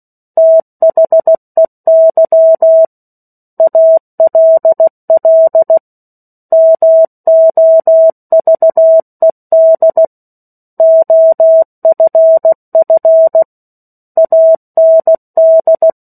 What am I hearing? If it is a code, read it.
Morse code, heard as a keyed tone, about 16 wpm, 650 Hz. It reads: THEY ALL MOVED OFF AND
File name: 03-Chapter-03 16 WPM